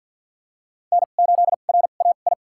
Morse code, heard as a keyed tone, about 47 wpm, spaced as for 35 wpm, 700 Hz. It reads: N8RAI
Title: USA Call Signs - 5 Characters - 2x Character Spacing - 50wpm